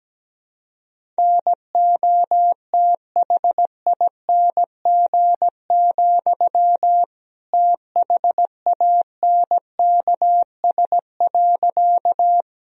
Morse code, heard as a keyed tone, about 17 wpm, 700 Hz. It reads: NOTHING, THANKS.
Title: Sentences from Top 1000 Words - Review Part 3 - 17wpm